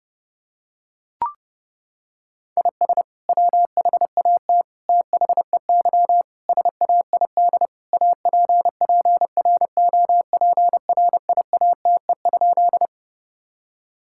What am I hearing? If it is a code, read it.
Morse code, heard as a keyed tone, about 28 wpm, 700 Hz. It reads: IS WHAT THEY SAID APPROPRIATE?
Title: Sentences from Top 1000 Words - Review Part 5 - 30wpm